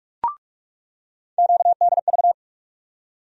Morse code, heard as a keyed tone, about 47 wpm, 700 Hz. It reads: QDV
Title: USA Call Sign Postfixes - 3 Letters 45wpm